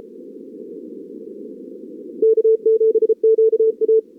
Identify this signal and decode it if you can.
Morse code, heard as a keyed tone, about 34 wpm, 440 Hz. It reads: K7QA
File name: CWSprint / Feb2022 / k5tr-feb-na-sprint-cw-k5ot -0000